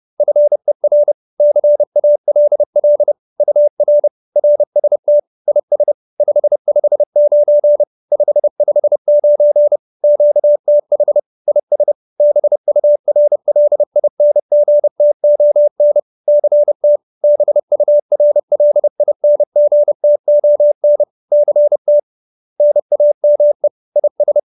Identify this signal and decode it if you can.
Morse code, heard as a keyed tone, about 30 wpm, 590 Hz. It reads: FER CALL UR RST IS 559 559 QTH IS BURLINGTON CT BURLINGTON CT NAME IS
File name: QSO308 30